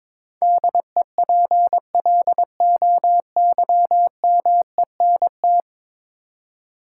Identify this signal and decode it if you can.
Morse code, heard as a keyed tone, about 22 wpm, 700 Hz. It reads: DEPLOYMENT
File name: practice / 22wpm / 10 Letter Words - First 3 Letters Given - 22wpm